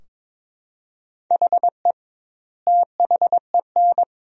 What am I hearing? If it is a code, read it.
Morse code, heard as a keyed tone, about 22 wpm, 700 Hz. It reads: HE THEN
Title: 2x Word-Spacing - Sentences from Top 300 Words - 22wpm